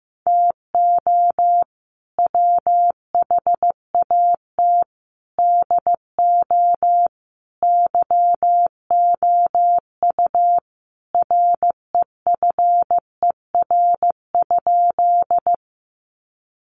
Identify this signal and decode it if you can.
Morse code, heard as a keyed tone, about 15 wpm, 700 Hz. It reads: TO WHAT DO YOU REFER?
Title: Sentences from Top 1000 Words 15wpm